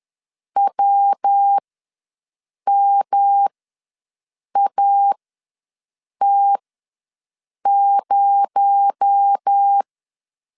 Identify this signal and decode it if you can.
Morse code, heard as a keyed tone, about 10 wpm, spaced as for 6 wpm, 800 Hz. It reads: WMAT0
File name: LEZIONI / 7 GF9Z / 7 GF ws 10wpm 3918